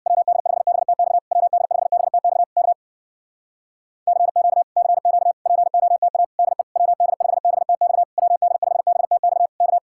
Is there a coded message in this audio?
Yes: Morse code from 0.1 to 9.8 s, heard as a keyed tone, about 64 wpm, 700 Hz. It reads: WD5BTX WD5BTX K CQ CQ POTA DE WD5BTX WD5BTX K